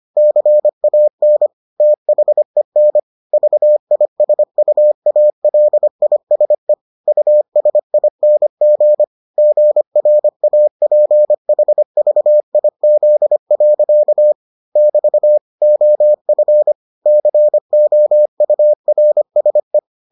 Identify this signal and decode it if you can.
Morse code, heard as a keyed tone, about 25 wpm, 600 Hz. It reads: CAN THEN VISUALISE USING GRAPHVIZ. <BT> OF COURSE